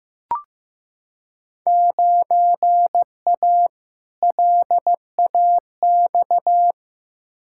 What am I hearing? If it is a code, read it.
Morse code, heard as a keyed tone, about 15 wpm, 700 Hz. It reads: 9A LAX
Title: Field Day Exchanges Encore - 15wpm